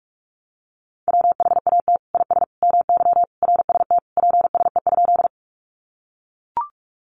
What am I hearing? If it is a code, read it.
Morse code, heard as a keyed tone, about 44 wpm, 700 Hz. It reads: WHAT IS MY RST PSE?
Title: Common QSO Phrases 45wpm